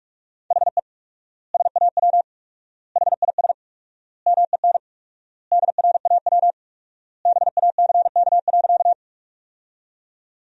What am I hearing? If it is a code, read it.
Morse code, heard as a keyed tone, about 46 wpm, 700 Hz. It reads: HE SAW HIS MEN DRAW BACK.
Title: 4x Word-Spacing - Sentences from Top 300 Words - 45wpm